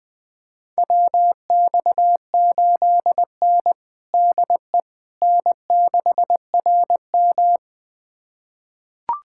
Morse code, heard as a keyed tone, about 21 wpm, 700 Hz. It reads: WX8N DE N6RM
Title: USA Callsign DE Callsign - 4 Characters - 20wpm